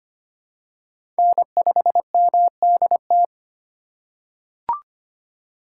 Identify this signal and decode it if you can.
Morse code, heard as a keyed tone, about 25 wpm, 700 Hz. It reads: N5MDT
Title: USA Call Signs - 3x Character Spacing - 25wpm